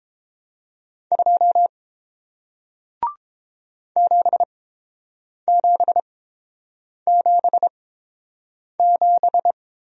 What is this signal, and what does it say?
Morse code, heard as a keyed tone, about 31 wpm, spaced as for 7 wpm, 700 Hz. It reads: 2 7777